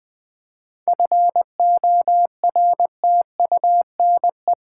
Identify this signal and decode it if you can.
Morse code, heard as a keyed tone, about 20 wpm, 700 Hz. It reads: FORTUNE
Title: 7 Letter Words - First 3 Letters Given - 20wpm